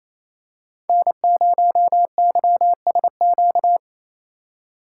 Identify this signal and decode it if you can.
Morse code, heard as a keyed tone, about 28 wpm, 700 Hz. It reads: N0YSQ
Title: K1USN SST QSO Practice - 28wpm